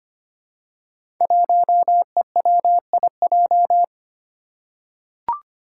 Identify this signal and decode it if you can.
Morse code, heard as a keyed tone, about 25 wpm, 700 Hz. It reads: 1EWIJ
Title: Five Challenging Letter-Number Combinations 25wpm